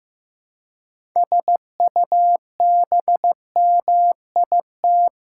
Morse code, heard as a keyed tone, about 15 wpm, 700 Hz. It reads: SUBMIT
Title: Six Letter Words - 15wpm